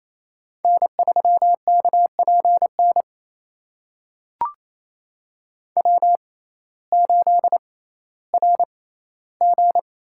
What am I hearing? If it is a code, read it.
Morse code, heard as a keyed tone, about 28 wpm, spaced as for 15 wpm, 700 Hz. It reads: N3KPN W8RG
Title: USA Call Signs - 6x Character Spacing - 28wpm